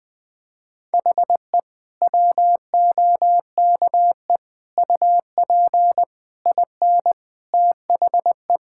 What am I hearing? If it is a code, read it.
Morse code, heard as a keyed tone, about 20 wpm, 700 Hz. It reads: HE WOKE UP IN THE
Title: Sentences with 7-Word Phrases - 20wpm